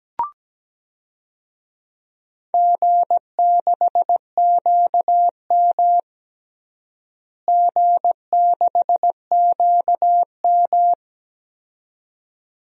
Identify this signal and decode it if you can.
Morse code, heard as a keyed tone, about 17 wpm, 700 Hz. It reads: G6QM G6QM
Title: DXCC-100 Call Signs - 4 Characters - Repeated 2x - 17wpm